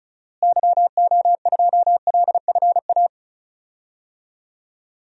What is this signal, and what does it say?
Morse code, heard as a keyed tone, about 36 wpm, 700 Hz. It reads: YO2LFA
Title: DXCC-100 Call Signs - 6 Characters - Repeated 2x - 35wpm